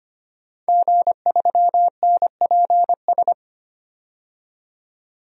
Morse code, heard as a keyed tone, about 25 wpm, 700 Hz. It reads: G3NPS
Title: DXCC-100 Call Signs - 5 Characters - 25wpm